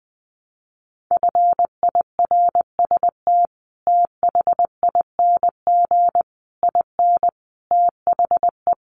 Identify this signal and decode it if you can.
Morse code, heard as a keyed tone, about 20 wpm, 700 Hz. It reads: FIRST THING IN THE
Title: Sets of 5 Words Encore2 20wpm